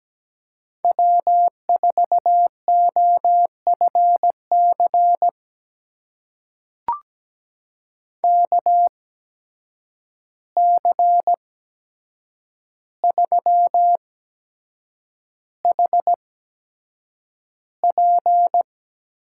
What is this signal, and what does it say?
Morse code, heard as a keyed tone, about 17 wpm, spaced as for 7 wpm, 700 Hz. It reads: W4OFC KC3HP
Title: USA Call Signs - 8x Character Spacing - 17wpm